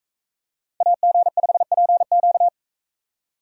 Morse code, heard as a keyed tone, about 44 wpm, 700 Hz. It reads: AG5PQ